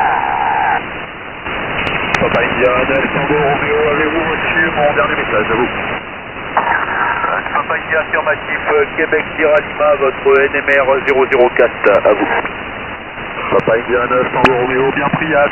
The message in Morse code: none

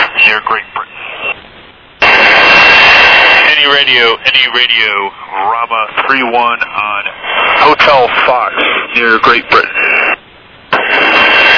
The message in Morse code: none

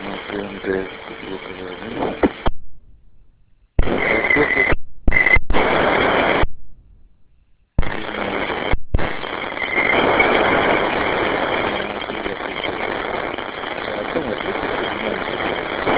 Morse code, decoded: none

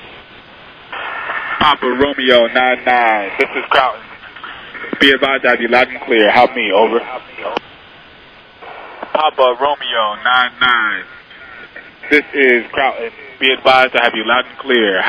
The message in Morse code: none

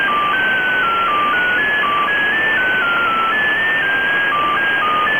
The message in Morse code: none

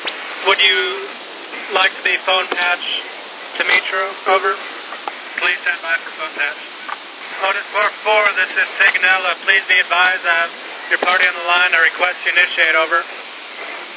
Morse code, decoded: none

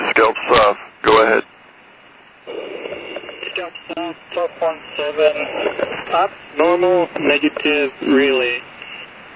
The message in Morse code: none

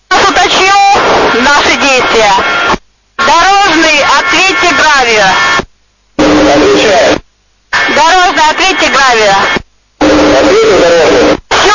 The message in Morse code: none

